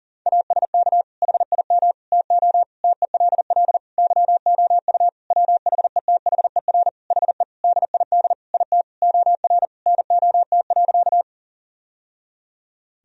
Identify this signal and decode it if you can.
Morse code, heard as a keyed tone, about 40 wpm, 700 Hz. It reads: ASK HIM TO TELL YOU WHETHER HE DID IT OR NOT.
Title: Sentences from Top 1000 Words - Review Part 3 - 40wpm